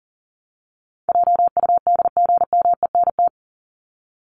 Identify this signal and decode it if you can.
Morse code, heard as a keyed tone, about 40 wpm, 700 Hz. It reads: JUDGMENT
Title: Top 1751-2000 Words 40wpm